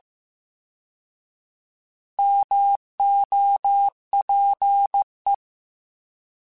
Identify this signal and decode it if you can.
Morse code, heard as a keyed tone, about 15 wpm, 800 Hz. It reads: MOPE